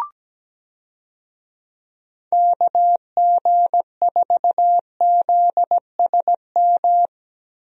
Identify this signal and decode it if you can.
Morse code, heard as a keyed tone, about 17 wpm, 700 Hz. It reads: KG4ZSM
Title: USA Call Signs - 6 Characters - Repeated 2x - 17wpm